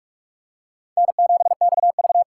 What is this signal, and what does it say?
Morse code, heard as a keyed tone, about 45 wpm, 700 Hz. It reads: N7XV